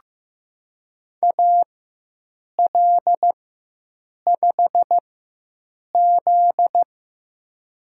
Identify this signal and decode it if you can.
Morse code, heard as a keyed tone, about 15 wpm, spaced as for 7 wpm, 700 Hz. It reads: AL5Z
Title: USA Call Signs - 4 Characters - 4x Character Spacing - 15wpm